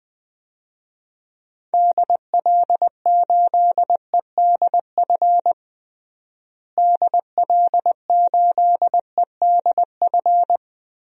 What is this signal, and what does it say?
Morse code, heard as a keyed tone, about 20 wpm, 700 Hz. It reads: DL8EDF DL8EDF
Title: DXCC-100 Call Signs - 6 Characters - Repeated 3x - 20wpm